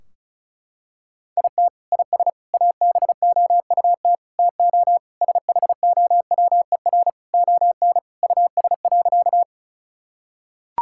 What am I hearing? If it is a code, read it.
Morse code, heard as a keyed tone, about 36 wpm, 700 Hz. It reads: IT IS ABOUT TO SHOWER ON US.
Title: Sentences from Top 1600 Words 35wpm